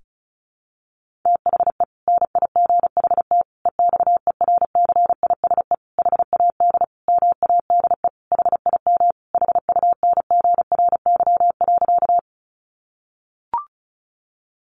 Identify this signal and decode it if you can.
Morse code, heard as a keyed tone, about 34 wpm, 700 Hz. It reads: THE NIGHT EXERCISE HAD MADE HIM HUNGRY.
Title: Sentences from Top 1400 Words 35wpm